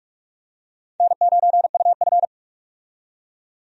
Morse code, heard as a keyed tone, about 47 wpm, 700 Hz. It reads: N9UF
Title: USA Call Signs - 4 Characters - 45wpm